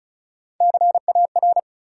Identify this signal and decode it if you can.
Morse code, heard as a keyed tone, about 36 wpm, 700 Hz. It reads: CAR
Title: Sets of 5 Words Encore - New Word Drill - 35wpm